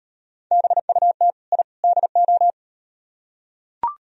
Morse code, heard as a keyed tone, about 38 wpm, 700 Hz. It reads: BUT I DO